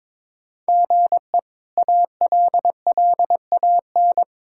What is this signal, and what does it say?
Morse code, heard as a keyed tone, about 21 wpm, 700 Hz. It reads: GE ALLAN
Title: K1USN SST QSO Practice - 22wpm